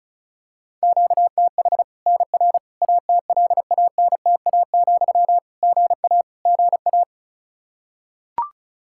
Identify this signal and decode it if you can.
Morse code, heard as a keyed tone, about 34 wpm, 700 Hz. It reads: QTH NR ATLANTA, GA GA
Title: K1USN SST QSO Practice 35wpm